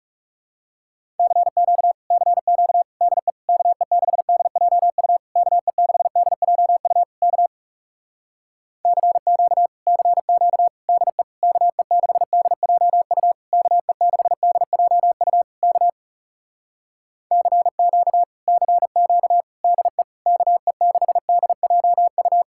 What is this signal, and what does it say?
Morse code, heard as a keyed tone, about 40 wpm, 700 Hz. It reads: CQ CQ DE KE6DJU KE6DJU K CQ CQ DE KE6DJU KE6DJU K CQ CQ DE KE6DJU